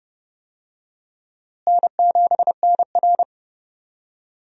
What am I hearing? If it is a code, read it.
Morse code, heard as a keyed tone, about 32 wpm, 700 Hz. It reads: N7NR
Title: USA Call Signs - 4 Characters - Repeated 2x - 30wpm